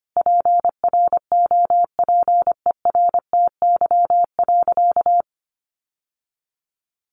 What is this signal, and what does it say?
Morse code, heard as a keyed tone, about 25 wpm, 700 Hz. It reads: PROPERTY.
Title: Sentences from Top 1000 Words 25wpm